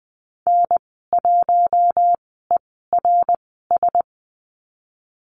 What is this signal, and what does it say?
Morse code, heard as a keyed tone, about 20 wpm, spaced as for 15 wpm, 700 Hz. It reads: N1ERS
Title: 055 Lesson 10g - Letter r - Call Signs 20wpm